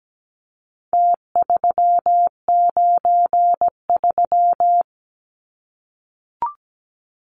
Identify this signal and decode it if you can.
Morse code, heard as a keyed tone, about 17 wpm, 700 Hz. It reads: T393
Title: Numbers - Contest Style - 4-Digit 17wpm